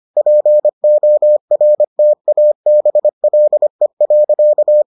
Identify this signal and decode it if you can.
Morse code, heard as a keyed tone, about 25 wpm, 600 Hz. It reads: PORTABLE.